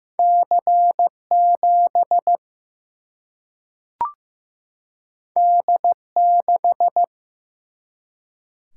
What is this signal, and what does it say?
Morse code, heard as a keyed tone, about 15 wpm, 700 Hz. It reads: C 7 D 6
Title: Single-Letter Single-Number 15wpm